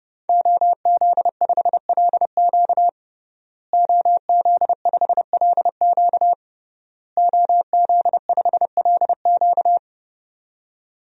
Morse code, heard as a keyed tone, about 30 wpm, 700 Hz. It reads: OZ5LQ OZ5LQ OZ5LQ
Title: DXCC-100 Call Signs - 5 Characters - Repeated 3x - 30wpm